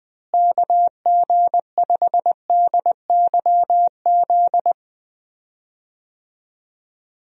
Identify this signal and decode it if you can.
Morse code, heard as a keyed tone, about 21 wpm, 700 Hz. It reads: KG5DYZ
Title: USA Call Signs - 6 Characters - Repeated 2x - 20wpm